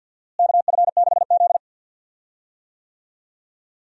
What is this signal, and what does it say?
Morse code, heard as a keyed tone, about 52 wpm, 700 Hz. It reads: KF6Z
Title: USA Call Signs - 4 Characters - Repeated 2x - 50wpm